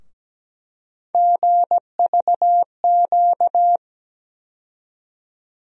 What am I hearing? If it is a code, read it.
Morse code, heard as a keyed tone, about 17 wpm, 700 Hz. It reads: GVQ
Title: USA Call Sign Postfixes - 3 Letters 17wpm